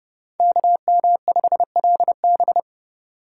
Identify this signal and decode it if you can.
Morse code, heard as a keyed tone, about 39 wpm, 700 Hz. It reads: KM5LB